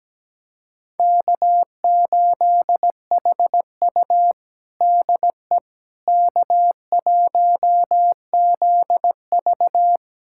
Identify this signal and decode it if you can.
Morse code, heard as a keyed tone, about 17 wpm, 700 Hz. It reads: K8HU DE K1ZV
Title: USA Callsign DE Callsign - 4 Characters - 17wpm